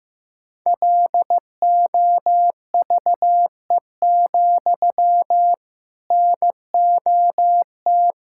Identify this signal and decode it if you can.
Morse code, heard as a keyed tone, about 15 wpm, 700 Hz. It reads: LOVE, NOT
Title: Sentences from Top 1600 Words 15wpm